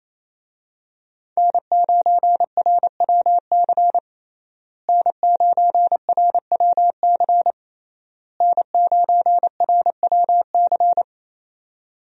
Morse code, heard as a keyed tone, about 28 wpm, 700 Hz. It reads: N9RWC N9RWC N9RWC
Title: USA Call Signs - 5 Characters - Repeated 3x - 28wpm